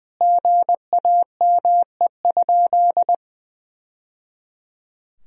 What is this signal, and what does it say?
Morse code, heard as a keyed tone, about 21 wpm, 700 Hz. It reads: GAME?
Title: Sentences from Top 1000 Words - Review Part 4 - 20wpm